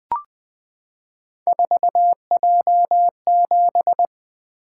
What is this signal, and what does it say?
Morse code, heard as a keyed tone, about 21 wpm, 700 Hz. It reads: 4J7
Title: Three Letter-Number Combinations - 20wpm